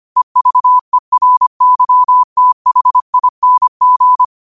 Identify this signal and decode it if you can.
Morse code, heard as a keyed tone, about 25 wpm, 1000 Hz. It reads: EVERYTHING
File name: qotdfiles / 25wpm